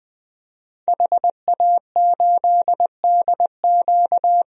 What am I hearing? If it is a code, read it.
Morse code, heard as a keyed tone, about 20 wpm, 700 Hz. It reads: HA8DQ
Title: DXCC-100 Call Signs - 5 Characters - Repeated 3x - 20wpm